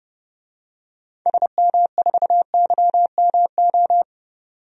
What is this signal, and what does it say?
Morse code, heard as a keyed tone, about 30 wpm, 700 Hz. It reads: SM4YMO
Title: DXCC-100 Call Signs - 6 Characters - 30wpm